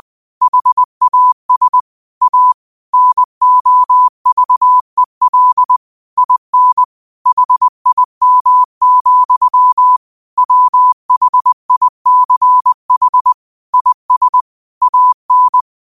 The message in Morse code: HAS A NOVEL IN HIM, WHICH IS AN

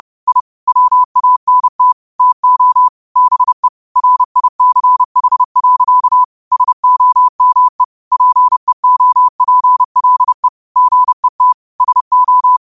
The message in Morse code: I WANT TO BE RICH. SOME PEOPLE GET SO

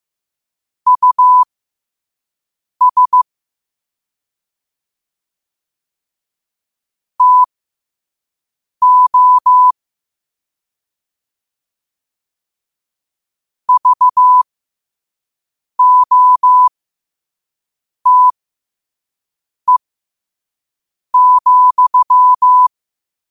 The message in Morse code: US TO VOTE,

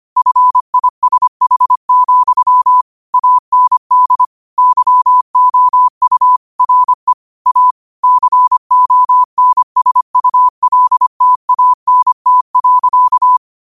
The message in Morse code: FISH, AND YOU RE A CONSULTANT.

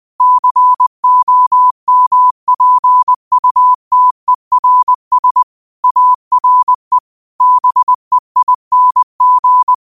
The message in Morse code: COMPUTERS ARE BEING

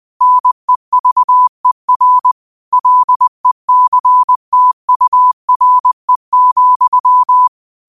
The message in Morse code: NEVER LECTURE,

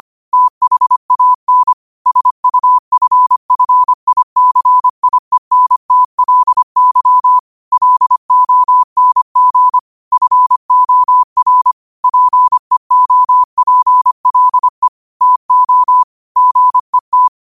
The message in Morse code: THAN SUFFICIENTLY LONG FOR PEOPLE TO GET

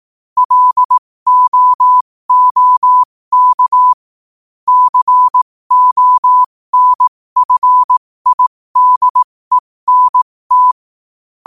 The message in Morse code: LOOK CONFIDENT

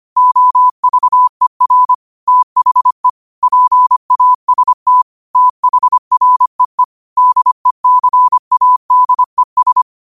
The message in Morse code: OVER THE PAST THREE DECADES

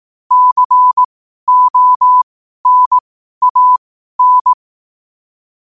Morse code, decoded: CONAN

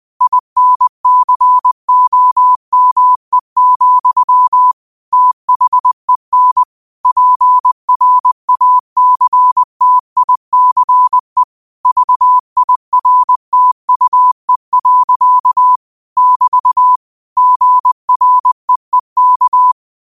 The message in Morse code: INCOME, THEN PRACTICE VIRTUE. <BT> GREEK

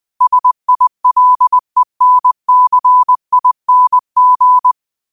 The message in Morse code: SILENCING

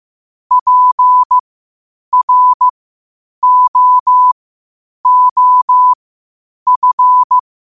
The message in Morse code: PROOF

